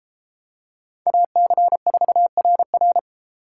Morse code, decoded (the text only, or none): AC4RR